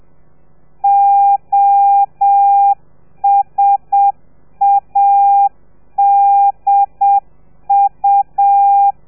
Morse code, heard as a keyed tone, about 7 wpm, 800 Hz. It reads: OSADU